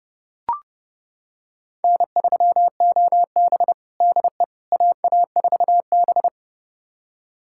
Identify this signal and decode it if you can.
Morse code, heard as a keyed tone, about 30 wpm, 700 Hz. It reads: N3OB DE AA4B
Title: USA Callsign DE Callsign - 4 Characters - 30wpm